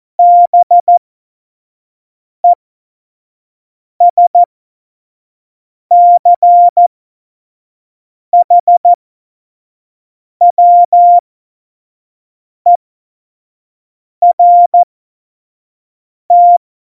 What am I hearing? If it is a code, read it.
Morse code, heard as a keyed tone, about 14 wpm, spaced as for 5 wpm, 700 Hz. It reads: BESCHWERT
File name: Morsekurs / Witze / Witz 10. September 2019, 25 F70